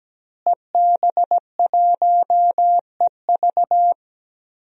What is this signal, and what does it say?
Morse code, heard as a keyed tone, about 17 wpm, 700 Hz. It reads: EB1EV
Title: DXCC-100 Call Signs - 5 Characters - Repeated 2x - 17wpm